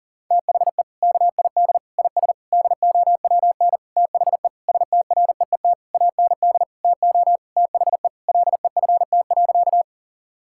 Morse code, heard as a keyed tone, about 40 wpm, 700 Hz. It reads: THE KID IS DOWN THE STREET AND TO THE LEFT.
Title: Sentences from Top 1600 Words 40wpm